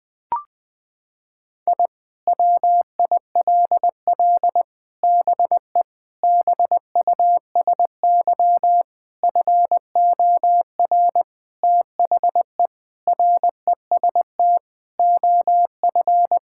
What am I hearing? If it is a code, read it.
Morse code, heard as a keyed tone, about 20 wpm, 700 Hz. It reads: I WILL BE BUSY FOR THE REST OF
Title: Sentences with 5-Word Phrases - 20wpm